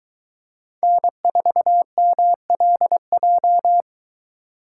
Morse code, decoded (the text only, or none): N4MLJ